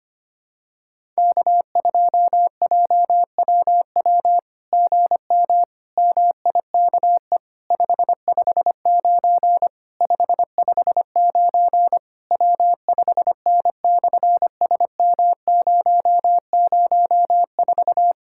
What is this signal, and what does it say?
Morse code, heard as a keyed tone, about 24 wpm, 700 Hz. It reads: K2JWW GM MIKE 559 559 W5N/SM004
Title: SOTA QSO Practice 25wpm